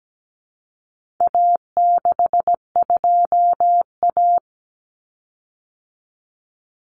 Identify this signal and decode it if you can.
Morse code, heard as a keyed tone, about 17 wpm, 700 Hz. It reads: A62A
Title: Common Cut Numbers - 4-Digit 17wpm